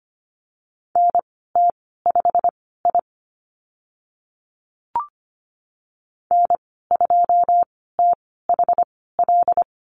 Morse code, heard as a keyed tone, about 25 wpm, spaced as for 16 wpm, 690 Hz. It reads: NT5I N2THL
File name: Lesson 15g - Number 5 - Call Signs 25wpm